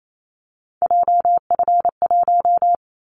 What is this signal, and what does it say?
Morse code, heard as a keyed tone, about 28 wpm, 700 Hz. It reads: JF1
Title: Callsign Prefixes for Top 25 DXCC - 28wpm